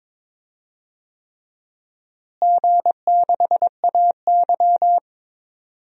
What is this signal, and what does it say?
Morse code, heard as a keyed tone, about 22 wpm, 700 Hz. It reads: G6AY
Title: DXCC-100 Call Signs - 4 Characters - Repeated 3x - 22wpm